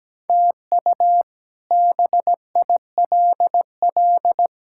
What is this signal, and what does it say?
Morse code, heard as a keyed tone, about 17 wpm, 700 Hz. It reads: TU BILL
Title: K1USN SST QSO Practice - 17wpm